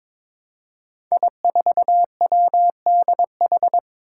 Morse code, heard as a keyed tone, about 22 wpm, 700 Hz. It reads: I4WDH
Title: Five Challenging Letter-Number Combinations 22wpm